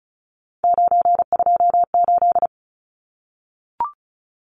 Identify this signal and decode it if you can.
Morse code, heard as a keyed tone, about 34 wpm, 700 Hz. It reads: 928
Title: Numbers - 3-Digit 35wpm